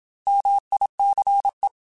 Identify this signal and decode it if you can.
Morse code, heard as a keyed tone, about 27 wpm, 800 Hz. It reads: MICE
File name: Alphabetical / Alphabetical mp3s / Ms @25